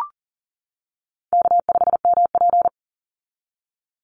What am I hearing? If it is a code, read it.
Morse code, heard as a keyed tone, about 38 wpm, 700 Hz. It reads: K5MP
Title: Easy Call Signs - 40wpm